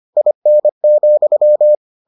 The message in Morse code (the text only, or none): IN,